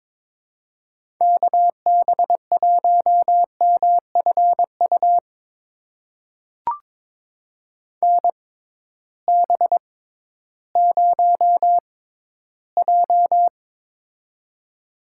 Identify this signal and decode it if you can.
Morse code, heard as a keyed tone, about 22 wpm, spaced as for 13 wpm, 700 Hz. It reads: KB1MFU NB0J
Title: USA Call Signs - 6x Character Spacing - 22wpm